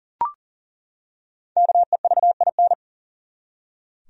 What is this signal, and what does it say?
Morse code, heard as a keyed tone, about 40 wpm, 700 Hz. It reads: KEVIN
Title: USA Common Names - Max 5 Letters - 40wpm